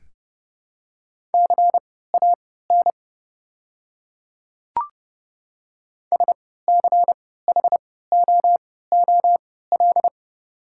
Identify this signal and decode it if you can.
Morse code, heard as a keyed tone, about 31 wpm, spaced as for 17 wpm, 700 Hz. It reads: CAN SCHOOL
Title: 3x Character Spacing - Top 500 Words - 30wpm